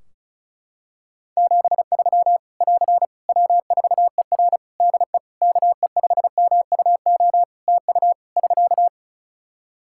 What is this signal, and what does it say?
Morse code, heard as a keyed tone, about 34 wpm, 700 Hz. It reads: 73 <AR> W4ER DE KE5MUO TU <SK>
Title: K1USN SST QSO Practice 35wpm